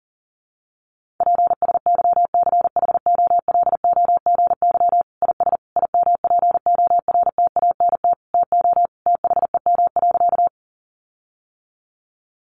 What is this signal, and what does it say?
Morse code, heard as a keyed tone, about 37 wpm, 700 Hz. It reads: PSYCHOLOGY IS IMPORTANT TO THEM.
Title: Sentences from Top 1400 Words 40wpm